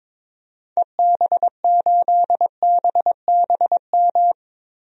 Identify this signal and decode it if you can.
Morse code, heard as a keyed tone, about 22 wpm, 700 Hz. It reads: EB8BBM
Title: DXCC-100 Call Signs - 6 Characters - Repeated 2x - 22wpm